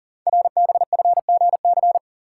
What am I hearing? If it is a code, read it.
Morse code, heard as a keyed tone, about 39 wpm, 690 Hz. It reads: RBFGC
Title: Code Groups - 5 Letters - 40wpm